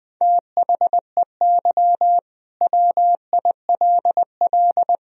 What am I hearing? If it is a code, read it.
Morse code, heard as a keyed tone, about 20 wpm, 700 Hz. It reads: THEY WILL
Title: Sentences from Top 1000 Words - Review Part 4 - 20wpm